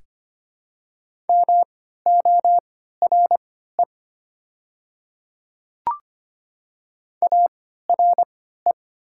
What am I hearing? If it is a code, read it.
Morse code, heard as a keyed tone, about 25 wpm, spaced as for 14 wpm, 700 Hz. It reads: MORE ARE
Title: 3x Character Spacing - Top 100 Words - 25wpm